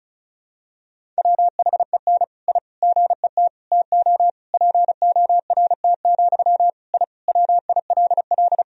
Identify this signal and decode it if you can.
Morse code, heard as a keyed tone, about 36 wpm, 700 Hz. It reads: WHEN I GET TO PORT, I WILL
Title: Sentences from Top 1000 Words - Review Part 1 - 35wpm